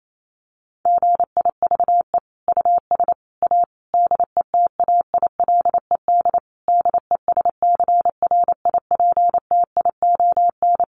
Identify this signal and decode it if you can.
Morse code, heard as a keyed tone, about 28 wpm, 700 Hz. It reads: GIVE US A DETAILED DESCRIPTION
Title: Sentences from Top 1500 Words 28wpm